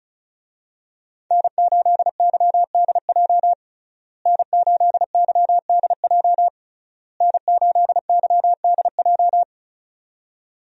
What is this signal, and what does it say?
Morse code, heard as a keyed tone, about 36 wpm, 700 Hz. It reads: N8YDJ N8YDJ N8YDJ
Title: USA Call Signs - 5 Characters - Repeated 3x - 35wpm